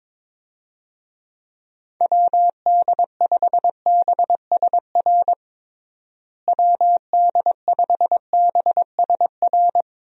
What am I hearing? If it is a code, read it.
Morse code, heard as a keyed tone, about 22 wpm, 700 Hz. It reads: WD5BSR WD5BSR
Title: USA Call Signs - 6 Characters - Repeated 3x - 22wpm